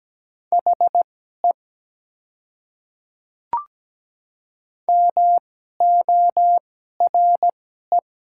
Morse code, heard as a keyed tone, about 17 wpm, spaced as for 12 wpm, 700 Hz. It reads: HE MORE
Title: 2x Character Spacing - Top 100 Words - 17wpm